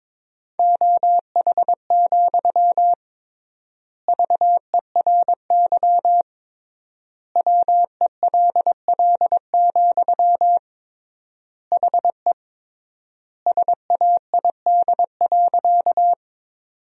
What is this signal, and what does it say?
Morse code, heard as a keyed tone, about 22 wpm, 700 Hz. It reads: OH, VERY WELL, HE SAID.
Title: 3x Word-Spacing - Sentences from Top 500 Words - 22wpm